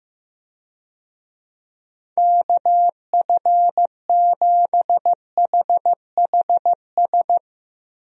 Morse code, heard as a keyed tone, about 15 wpm, 700 Hz. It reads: KF7HHS
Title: USA Call Signs - 6 Characters - Repeated 2x - 15wpm